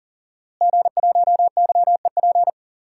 Morse code, heard as a keyed tone, about 40 wpm, 700 Hz. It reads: G1YEP
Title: DXCC-100 Call Signs - 5 Characters - Repeated 2x - 40wpm